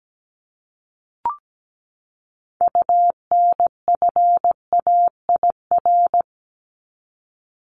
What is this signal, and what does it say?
Morse code, heard as a keyed tone, about 17 wpm, 700 Hz. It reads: UNFAIR
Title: Top 1751-2000 Words 17wpm